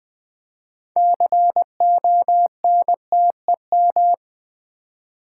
Morse code, heard as a keyed tone, about 20 wpm, 700 Hz. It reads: CONTEM
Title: Build A Word - 6 Letters At A Time - 12-Letter Words - 20wpm